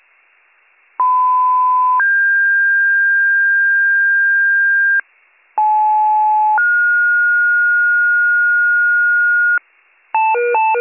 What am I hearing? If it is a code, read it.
Two-tone page, heard as a one-second tone, then a three-second tone, about 1010 then 1600 Hz, about 860 then 1430 Hz, with noise. It is heard from 1.0 to 9.6 s.